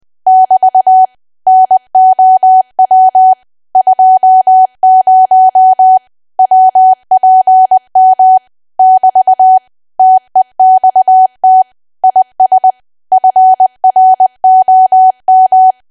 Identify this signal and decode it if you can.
Morse code, heard as a keyed tone, about 20 wpm, 750 Hz. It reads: <BT> NOW 20 WPM <BT> TEXT IS FROM
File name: cwa / W1AW-Level3 / 20WPM